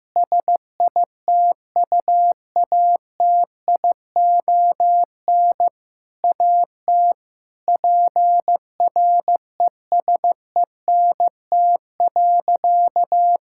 Morse code, heard as a keyed tone, about 16 wpm, 700 Hz. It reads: SITUATION AT PRESENT.
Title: Sentences from Top 1600 Words 15wpm